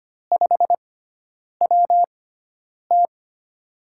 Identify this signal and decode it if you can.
Morse code, heard as a keyed tone, about 25 wpm, spaced as for 9 wpm, 700 Hz. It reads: 5WT
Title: USA Call Signs - 4 Characters - 6x Character Spacing - 25wpm